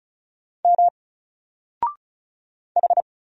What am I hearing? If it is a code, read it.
Morse code, heard as a keyed tone, about 36 wpm, 700 Hz. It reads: M H
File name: Single-Letter-Number-35wpm